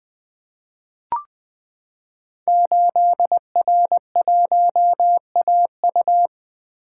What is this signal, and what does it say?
Morse code, heard as a keyed tone, about 20 wpm, 700 Hz. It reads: 8R1AU
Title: Code Groups - 5 Letter:Numbers - 20wpm